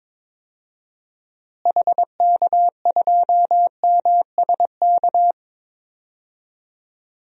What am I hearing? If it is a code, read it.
Morse code, heard as a keyed tone, about 22 wpm, 700 Hz. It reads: HK2MSK
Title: DXCC-100 Call Signs - 6 Characters - Repeated 3x - 22wpm